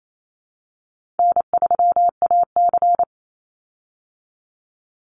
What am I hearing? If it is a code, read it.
Morse code, heard as a keyed tone, about 28 wpm, 700 Hz. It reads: N3AC
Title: Easy Call Signs Repeated 3x - 28wpm